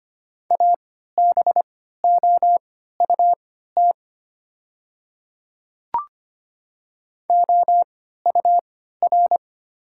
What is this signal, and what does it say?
Morse code, heard as a keyed tone, about 25 wpm, spaced as for 14 wpm, 700 Hz. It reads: ABOUT OUR
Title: 3x Character Spacing - Top 300 Words - 25wpm